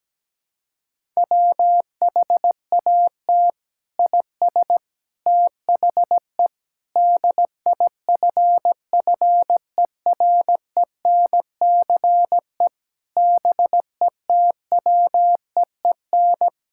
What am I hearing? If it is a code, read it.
Morse code, heard as a keyed tone, about 17 wpm, 700 Hz. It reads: WHAT IS THE DIFFERENCE BETWEEN
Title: Sentences from Top 1000 Words - Review Part 5 - 17wpm